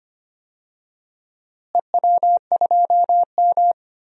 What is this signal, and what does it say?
Morse code, heard as a keyed tone, about 25 wpm, 700 Hz. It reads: EW2M
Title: DXCC-100 Call Signs - 4 Characters - Repeated 2x - 25wpm